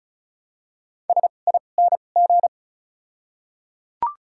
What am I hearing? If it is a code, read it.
Morse code, heard as a keyed tone, about 36 wpm, 700 Hz. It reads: S I N G
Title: 2x Character Spacing - Top 500 Words - 35wpm